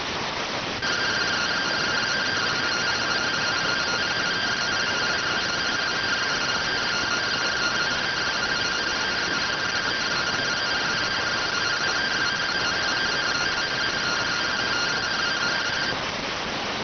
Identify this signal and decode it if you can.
Two-tone teleprinter signal, about 170 Hz shift.